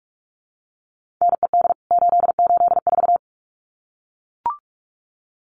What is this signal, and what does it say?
Morse code, heard as a keyed tone, about 44 wpm, 700 Hz. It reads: NED 884
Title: CWT Exchange 45wpm